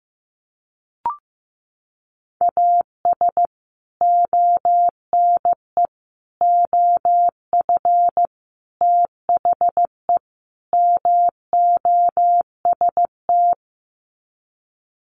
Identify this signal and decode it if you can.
Morse code, heard as a keyed tone, about 15 wpm, 700 Hz. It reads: AS ONE OF THE MOST
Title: Sets of 5 Words 15wpm